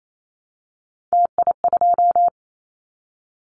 Morse code, heard as a keyed tone, about 28 wpm, 700 Hz. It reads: TI2
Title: Callsign Prefixes for Top 100 DXCC - 28wpm